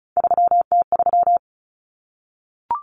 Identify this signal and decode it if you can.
Morse code, heard as a keyed tone, about 34 wpm, 700 Hz. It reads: 3T3
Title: Common Cut Numbers - 3-Digit 35wpm